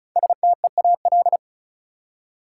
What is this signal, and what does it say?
Morse code, heard as a keyed tone, about 36 wpm, 700 Hz. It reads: STEAL
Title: Five Letter Words - 35wpm